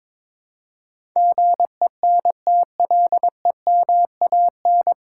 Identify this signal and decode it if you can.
Morse code, heard as a keyed tone, about 22 wpm, 700 Hz. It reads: GENTLEMAN
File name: 9 Letter Words - First 3 Letters Given - 22wpm